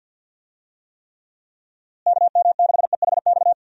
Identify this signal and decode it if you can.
Morse code, heard as a keyed tone, about 47 wpm, 700 Hz. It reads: KM6EHX
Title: USA Call Signs - 6 Characters - Repeated 2x - 50wpm